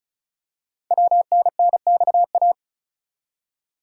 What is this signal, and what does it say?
Morse code, heard as a keyed tone, about 36 wpm, 700 Hz. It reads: WNNXA
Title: Code Groups - 5 Letters - 35wpm